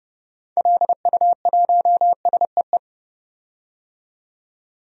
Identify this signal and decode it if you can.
Morse code, heard as a keyed tone, about 29 wpm, 700 Hz. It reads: LU1SEE
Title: DXCC-100 Call Signs - 6 Characters - 30wpm